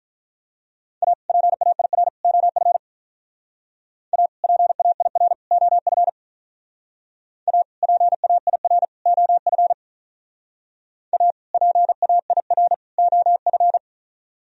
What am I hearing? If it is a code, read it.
Morse code, heard as a keyed tone, about 43 wpm, 700 Hz. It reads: A PAIR OF A PAIR OF A PAIR OF A PAIR OF
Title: Speed-Racing - Sets of 3 Words 35wpm